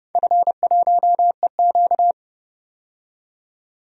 Morse code, heard as a keyed tone, about 29 wpm, 700 Hz. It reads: F1EQ
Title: DXCC-100 Call Signs - 4 Characters - 30wpm